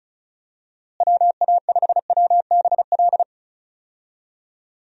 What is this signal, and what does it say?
Morse code, heard as a keyed tone, about 34 wpm, 700 Hz. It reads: WA5WBL
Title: USA Call Signs - 6 Characters - Repeated 3x - 35wpm